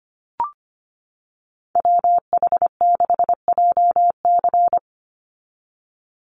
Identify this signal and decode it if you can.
Morse code, heard as a keyed tone, about 25 wpm, 700 Hz. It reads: WH6JC
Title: USA Call Signs 25wpm